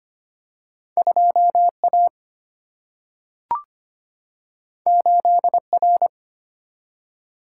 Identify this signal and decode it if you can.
Morse code, heard as a keyed tone, about 25 wpm, 700 Hz. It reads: 2A 8R
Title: Single-Letter Single-Number 25wpm